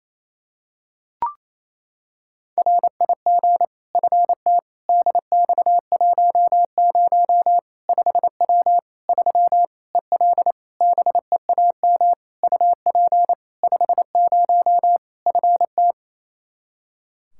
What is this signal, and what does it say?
Morse code, heard as a keyed tone, about 27 wpm, 700 Hz. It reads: RIG FT DX10 5W 3 EL BEAM UP 50 FT
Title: Practice QSO - Exchange 2 - Info - 28wpm